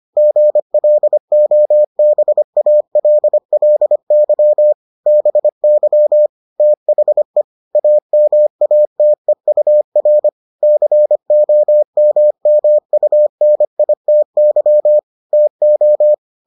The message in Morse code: GLOBALLY BY THE AMATEUR COMMUNITY TO

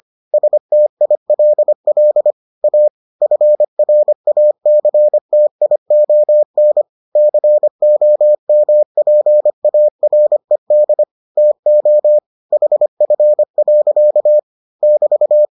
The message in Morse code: STILL A FRACTION COMPARED TO HF. <BT>

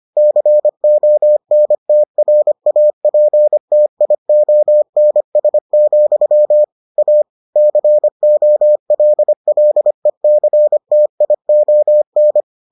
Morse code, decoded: CONTRAPTIONS, A COLLECTION

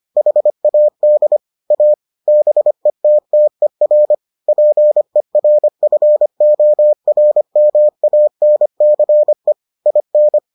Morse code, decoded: HAD A BETTER PERFORMANCE IN

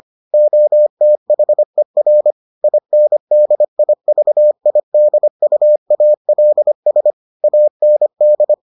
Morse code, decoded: OTHER INDIVIDUALS AND

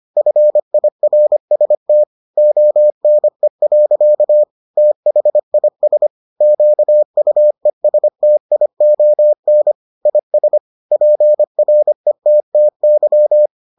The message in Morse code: FIRST ONE. THIS QUESTION IS PRETTY

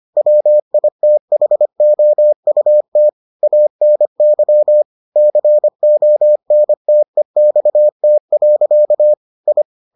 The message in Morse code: WITHOUT ANY CONTEXT. I

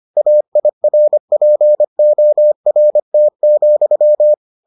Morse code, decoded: AIRPORT,

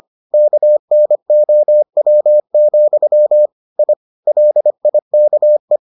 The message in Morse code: KNOW, I LIKE

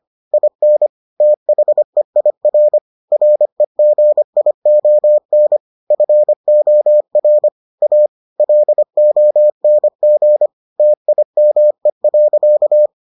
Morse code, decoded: IN THEIR REGION FOR A LONG TIME.

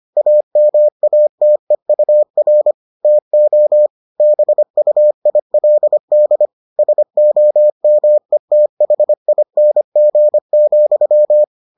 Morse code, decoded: AMATEUR TO BUILD SOMETHING,

